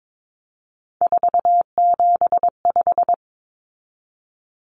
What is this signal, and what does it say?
Morse code, heard as a keyed tone, about 22 wpm, 700 Hz. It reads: 475
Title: Common Cut Numbers - 3-Digit 22wpm